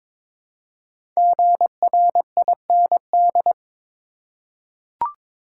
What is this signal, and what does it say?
Morse code, heard as a keyed tone, about 22 wpm, 700 Hz. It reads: GRIND
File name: 5 Letter Words - First 3 Letters Given - 22wpm